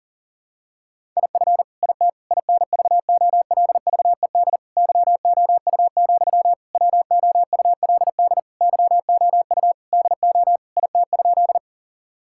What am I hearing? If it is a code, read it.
Morse code, heard as a keyed tone, about 40 wpm, 700 Hz. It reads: IF IT INVOLVED YOU, WOULD YOU DO IT?
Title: Sentences from Top 1000 Words - Review Part 5 - 40wpm